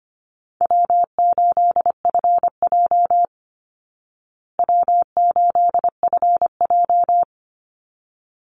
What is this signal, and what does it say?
Morse code, heard as a keyed tone, about 25 wpm, 700 Hz. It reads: W8FJ W8FJ
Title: Easy Call Signs Repeated 2x - 25wpm